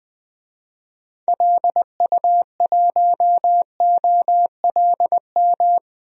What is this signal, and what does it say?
Morse code, heard as a keyed tone, about 20 wpm, 700 Hz. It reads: LU1OLM
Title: DXCC-100 Call Signs - 6 Characters - Repeated 3x - 20wpm